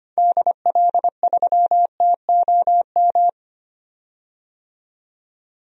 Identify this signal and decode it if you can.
Morse code, heard as a keyed tone, about 25 wpm, 700 Hz. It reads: DL3TOM
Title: DXCC-100 Call Signs - 6 Characters - 25wpm